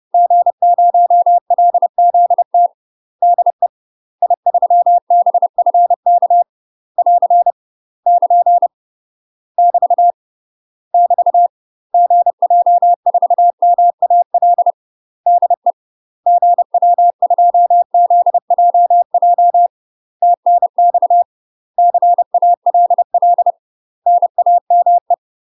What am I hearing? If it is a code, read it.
Morse code, heard as a keyed tone, about 30 wpm, 700 Hz. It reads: G0LZT DE I3BFK <AR> <KN> <BT> <BT> GJ4MAL DE GW2ZJJ TNX CALL NAME
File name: UK MASTER QSOs TRAINING / 30WPM / UK MASTER QSOs 0027